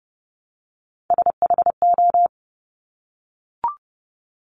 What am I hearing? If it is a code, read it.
Morse code, heard as a keyed tone, about 30 wpm, 690 Hz. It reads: SHO
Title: Top Three Letter Combinations in Words 30wpm